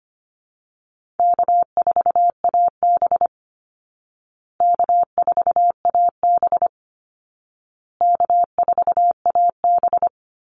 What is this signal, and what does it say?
Morse code, heard as a keyed tone, about 25 wpm, 700 Hz. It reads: K4AB K4AB K4AB
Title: Call Signs Repeated 2x - 25wpm